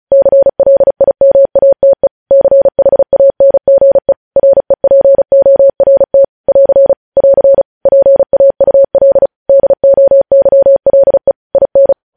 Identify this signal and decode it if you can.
Morse code, heard as a keyed tone, about 36 wpm, 550 Hz. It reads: CLIMATE CHANGE REPORT <AR> <AR> PAUL DOYLE IN
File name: podcast / News headlines-1x-35x35